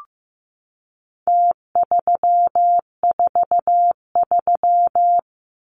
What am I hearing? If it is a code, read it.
Morse code, heard as a keyed tone, about 15 wpm, 700 Hz. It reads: T343
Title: Numbers - Contest Style - 4-Digit 15wpm